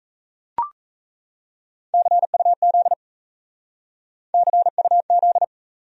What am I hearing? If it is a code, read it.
Morse code, heard as a keyed tone, about 33 wpm, 700 Hz. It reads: CUZ CUZ
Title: Speed-Racing Abbreviations - 28wpm